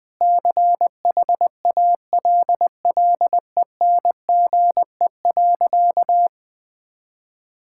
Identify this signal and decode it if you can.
Morse code, heard as a keyed tone, about 20 wpm, 700 Hz. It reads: CHALLENGE.
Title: Sentences with 5-Word Phrases - Spoken Beginning - 20wpm